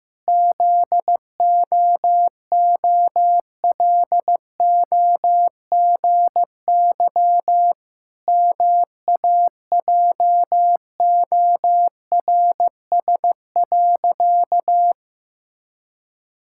Morse code, heard as a keyed tone, about 15 wpm, 700 Hz. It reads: ZOOLOGY MAJORS.
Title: Pangram Challenge v2 - 15wpm